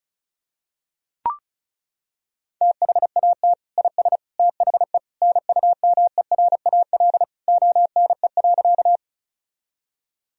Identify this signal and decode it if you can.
Morse code, heard as a keyed tone, about 34 wpm, 700 Hz. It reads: THAT IS THE NUMERAL ONE.
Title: Sentences from Top 1000 Words - Review Part 2 - 35wpm